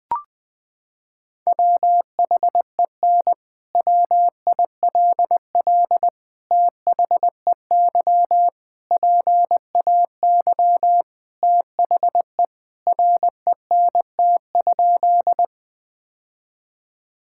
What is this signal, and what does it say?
Morse code, heard as a keyed tone, about 20 wpm, 700 Hz. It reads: WHEN WILL THEY PAY THE RENT?
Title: Sentences from Top 1000 Words - Review Part 5 - 20wpm